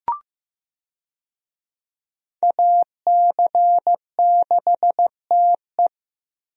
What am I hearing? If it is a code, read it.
Morse code, heard as a keyed tone, about 16 wpm, 700 Hz. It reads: AC6TE
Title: USA Call Signs - 5 Characters - Repeated 2x - 15wpm